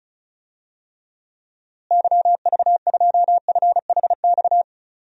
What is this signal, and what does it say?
Morse code, heard as a keyed tone, about 34 wpm, 700 Hz. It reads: YV2FHX
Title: DXCC-100 Call Signs - 6 Characters - Repeated 2x - 35wpm